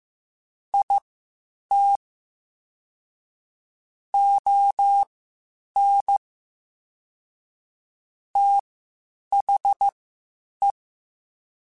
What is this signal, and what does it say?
Morse code, heard as a keyed tone, about 15 wpm, spaced as for 8 wpm, 800 Hz. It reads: IT ON THE
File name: Spaced / Chuckles Spaced-mp3s / HumanErr sp @13